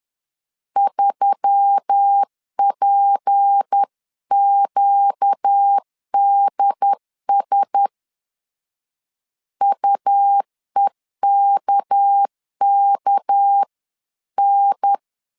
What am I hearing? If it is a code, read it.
Morse code, heard as a keyed tone, about 10 wpm, 800 Hz. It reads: 3PQDS UEKKN
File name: LEZIONI / 6 PKX0 / 6 PK 10wpm 2436